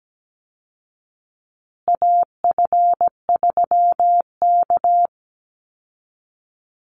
Morse code, heard as a keyed tone, about 17 wpm, 700 Hz. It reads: AF3K
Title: Call Signs Repeated 2x - 17wpm